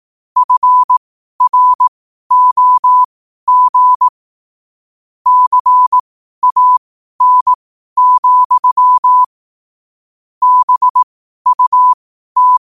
Morse code, heard as a keyed tone, about 18 wpm, spaced as for 12 wpm, 1000 Hz. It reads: FROG CAN, BUT
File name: 2025-11-08-12wpm